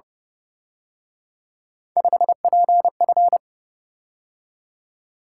Morse code, heard as a keyed tone, about 28 wpm, 700 Hz. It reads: 5PF